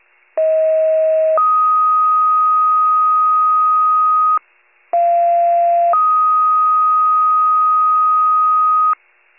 Two-tone page, heard as a one-second tone, then a three-second tone, about 640 then 1250 Hz, about 690 then 1250 Hz. It is heard from 0.4 to 8.9 s.